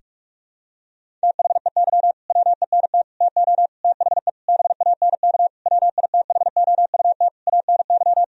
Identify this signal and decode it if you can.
Morse code, heard as a keyed tone, about 42 wpm, 700 Hz. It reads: THEY WENT TO THE BANK WITHOUT ANY
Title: Sentences from Top 1000 Words - Review Part 4 - 45wpm